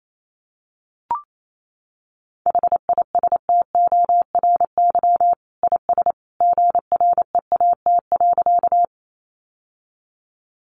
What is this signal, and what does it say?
Morse code, heard as a keyed tone, about 28 wpm, 700 Hz. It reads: HISTORY IS GREAT.
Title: Sentences from Top 600 Words 28wpm